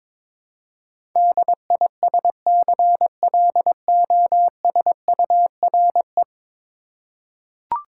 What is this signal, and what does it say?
Morse code, heard as a keyed tone, about 22 wpm, 700 Hz. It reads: DISCLOSURE
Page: morse-code-ninja-practice-22wpm